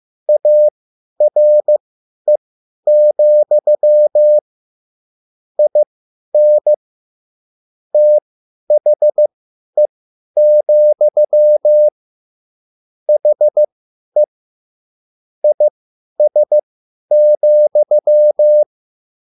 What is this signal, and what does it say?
Morse code, heard as a keyed tone, about 15 wpm, spaced as for 10 wpm, 600 Hz. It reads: ARE, IN THE, HE IS,
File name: QSL / CW lesson